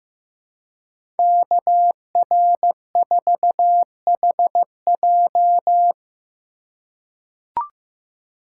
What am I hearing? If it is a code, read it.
Morse code, heard as a keyed tone, about 16 wpm, 700 Hz. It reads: KR4HJ
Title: USA Call Signs - 5 Characters - 2x Character Spacing - 15wpm